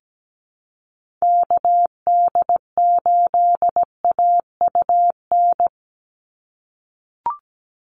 Morse code, heard as a keyed tone, about 17 wpm, 700 Hz. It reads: KD8AUN
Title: USA Call Signs 17wpm